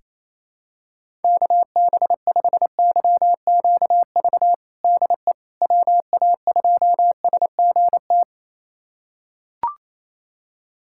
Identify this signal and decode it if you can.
Morse code, heard as a keyed tone, about 28 wpm, 700 Hz. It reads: KB5YQV DE WA2SGT
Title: USA Callsign DE Callsign 28wpm